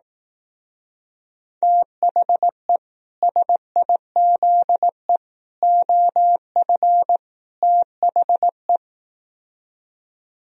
Morse code, heard as a keyed tone, about 18 wpm, 700 Hz. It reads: THE SIZE OF THE
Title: Speed-Racing - Sets of 4 Words 15wpm